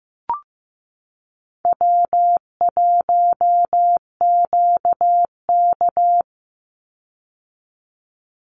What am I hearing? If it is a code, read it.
Morse code, heard as a keyed tone, about 15 wpm, 700 Hz. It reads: W1QK
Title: Easy Call Signs - 15wpm